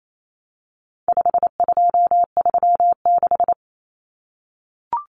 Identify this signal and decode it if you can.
Morse code, heard as a keyed tone, about 28 wpm, 700 Hz. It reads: 5236
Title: Numbers - 4-Digit 28wpm